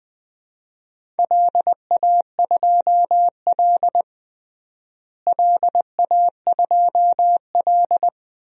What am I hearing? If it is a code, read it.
Morse code, heard as a keyed tone, about 20 wpm, 700 Hz. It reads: LA2L LA2L
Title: DXCC-100 Call Signs - 4 Characters - Repeated 2x - 20wpm